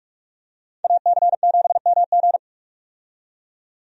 Morse code, heard as a keyed tone, about 46 wpm, 700 Hz. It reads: AC7MG
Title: USA Call Signs - 5 Characters - 45wpm